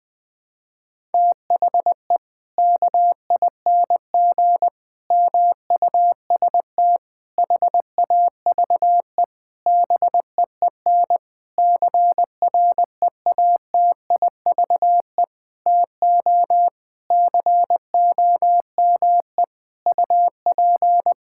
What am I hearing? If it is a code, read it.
Morse code, heard as a keyed tone, about 20 wpm, 700 Hz. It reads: THE KING MUST HAVE BEEN CREATIVE TO COME UP
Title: Sentences from Top 1000 Words - Review Part 5 - 20wpm